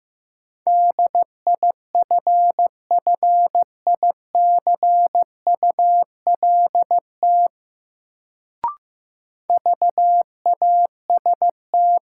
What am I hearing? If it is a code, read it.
Morse code, heard as a keyed tone, about 16 wpm, 700 Hz. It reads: DIFFICULT VAST
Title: Sets of 4 Words - New Word Drill - 15wpm